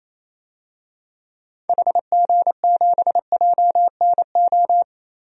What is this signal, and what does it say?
Morse code, heard as a keyed tone, about 28 wpm, 700 Hz. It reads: HG7JNO
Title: DXCC-100 Call Signs - 6 Characters - Repeated 2x - 28wpm